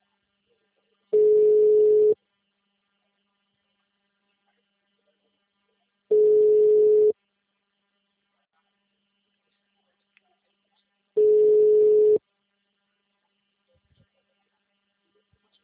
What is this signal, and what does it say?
Ringback tone (Europe, 425 Hz).